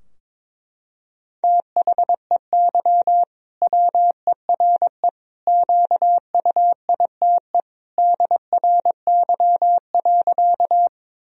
Morse code, heard as a keyed tone, about 22 wpm, 700 Hz. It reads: THEY WERE QUITE DRY.
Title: Sentences from Top 1000 Words - Review Part 3 - 22wpm